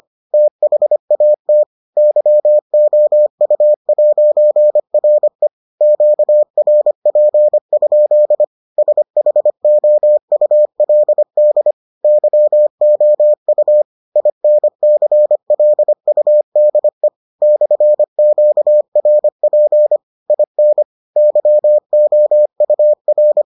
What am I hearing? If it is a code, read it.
Morse code, heard as a keyed tone, about 25 wpm, 600 Hz. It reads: THAT YOU'RE QRP? SHOULD YOU INCLUDE /QRP IN YOUR